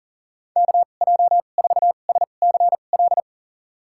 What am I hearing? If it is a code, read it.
Morse code, heard as a keyed tone, about 40 wpm, spaced as for 29 wpm, 700 Hz. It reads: KJ4SCL
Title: USA Call Signs - 6 Characters - 2x Character Spacing - 40wpm